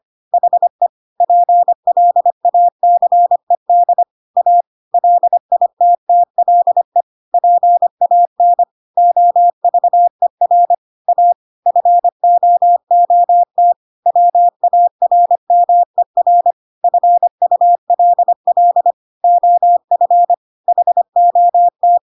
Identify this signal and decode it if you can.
Morse code, heard as a keyed tone, about 25 wpm, 700 Hz. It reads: HE PLACED A LITTLE PAN OVER A FOOT WARMER FULL OF HOT
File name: Pinocchio / Pinocchio ch5 25 25 ik1hge